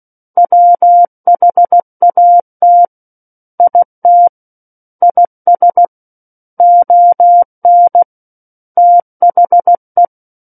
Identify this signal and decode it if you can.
Morse code, heard as a keyed tone, about 16 wpm, 700 Hz. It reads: WHAT IT IS ON THE
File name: Book One / 07-Chapter 7 - 16 WPM